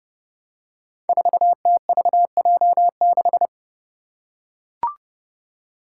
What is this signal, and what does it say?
Morse code, heard as a keyed tone, about 30 wpm, 700 Hz. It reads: 4TVJ6
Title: Five Challenging Letter-Number Combinations 30wpm